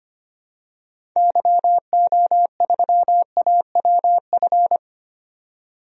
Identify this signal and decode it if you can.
Morse code, heard as a keyed tone, about 25 wpm, 700 Hz. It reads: YO3AWF
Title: DXCC-100 Call Signs - 6 Characters - 25wpm